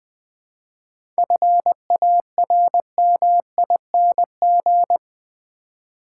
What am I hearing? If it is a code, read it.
Morse code, heard as a keyed tone, about 20 wpm, 700 Hz. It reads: FARMING
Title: 7 Letter Words - First 3 Letters Given - 20wpm